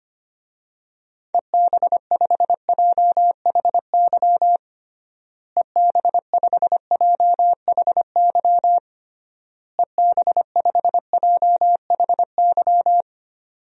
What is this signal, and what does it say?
Morse code, heard as a keyed tone, about 25 wpm, 700 Hz. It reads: EB5JHY EB5JHY EB5JHY
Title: DXCC-100 Call Signs - 6 Characters - Repeated 3x - 25wpm